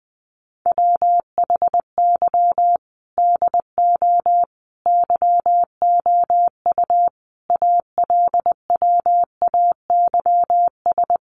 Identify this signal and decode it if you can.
Morse code, heard as a keyed tone, about 20 wpm, 700 Hz. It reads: WHY DO YOU ALWAYS
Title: Sentences from Top 1300 Words 20wpm